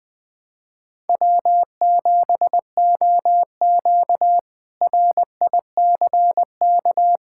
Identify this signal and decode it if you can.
Morse code, heard as a keyed tone, about 20 wpm, 700 Hz. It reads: W7OQ RICK
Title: K1USN SST QSO Practice - 20wpm